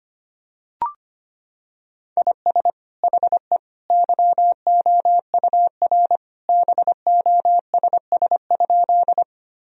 Morse code, heard as a keyed tone, about 25 wpm, 700 Hz. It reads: IS HE YOUR BOSS?
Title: Sentences from Top 1000 Words - Review Part 4 - 25wpm